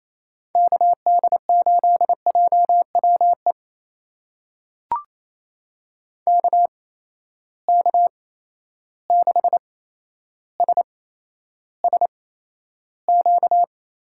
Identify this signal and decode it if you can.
Morse code, heard as a keyed tone, about 28 wpm, spaced as for 12 wpm, 700 Hz. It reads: KD8JWE KK6SSQ
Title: USA Call Signs - 6 Characters - 8x Character Spacing - 28wpm